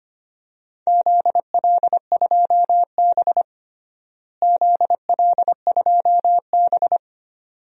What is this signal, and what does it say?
Morse code, heard as a keyed tone, about 25 wpm, 700 Hz. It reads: ZL2B ZL2B
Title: DXCC-100 Call Signs - 4 Characters - Repeated 2x - 25wpm